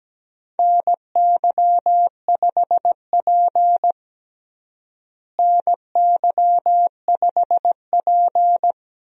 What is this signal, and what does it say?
Morse code, heard as a keyed tone, about 17 wpm, 700 Hz. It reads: NY5P NY5P
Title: USA Call Signs - 4 Characters - Repeated 2x - 17wpm